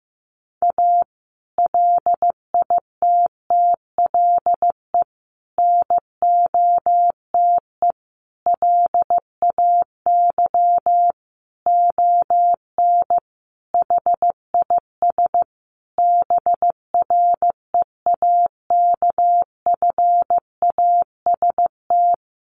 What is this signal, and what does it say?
Morse code, heard as a keyed tone, about 15 wpm, 700 Hz. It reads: A LITTLE NOTE LAY ON HIS BREAKFAST
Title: Sentences from Top 1500 Words 15wpm